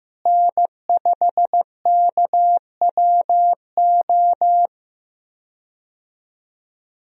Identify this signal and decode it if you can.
Morse code, heard as a keyed tone, about 15 wpm, 700 Hz. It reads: N5KWO
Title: USA Call Signs - 5 Characters - Repeated 3x - 15wpm